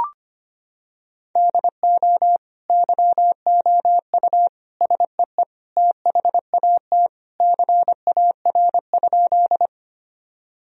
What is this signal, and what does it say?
Morse code, heard as a keyed tone, about 25 wpm, 700 Hz. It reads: DO YOU SEE THAT CAR?
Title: Sentences from Top 1000 Words - Review Part 1 - 25wpm